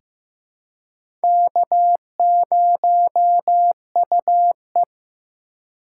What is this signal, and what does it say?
Morse code, heard as a keyed tone, about 15 wpm, 700 Hz. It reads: K0UE
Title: USA Call Signs - 4 Characters - 8x Character Spacing - 15wpm